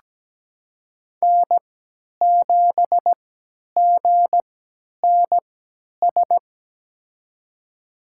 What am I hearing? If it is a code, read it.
Morse code, heard as a keyed tone, about 17 wpm, spaced as for 10 wpm, 700 Hz. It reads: N7GNS
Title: USA Call Signs - 5 Characters - 3x Character Spacing - 17wpm